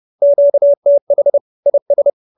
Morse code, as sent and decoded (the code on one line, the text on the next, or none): --.- - .... / .. ...
QTH IS